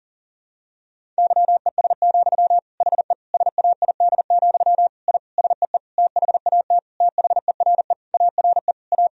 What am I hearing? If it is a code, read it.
Morse code, heard as a keyed tone, about 40 wpm, 700 Hz. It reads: YES, HE SAID, I SEE THAT THERE ARE A
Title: Sentences from Top 1000 Words - Review Part 1 - 40wpm